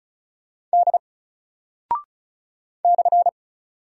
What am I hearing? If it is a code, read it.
Morse code, heard as a keyed tone, about 36 wpm, 700 Hz. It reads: D /
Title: Single Character (Letter:Number:Punctuation:Prosign) 35wpm